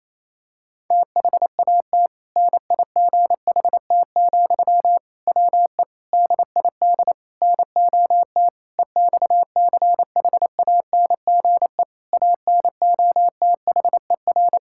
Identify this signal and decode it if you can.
Morse code, heard as a keyed tone, about 28 wpm, 700 Hz. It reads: THAT NIGHT, WE DID NOT EXCHANGE ANOTHER
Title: Sentences from Top 1000 Words - Review Part 6 - 28wpm